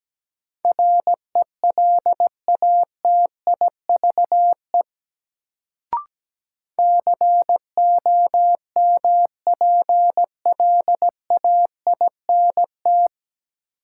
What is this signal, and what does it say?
Morse code, heard as a keyed tone, about 17 wpm, 700 Hz. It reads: RELATIVE COMPLAINT
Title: Sentences from Top 1400 Words - New Word Drill - 17wpm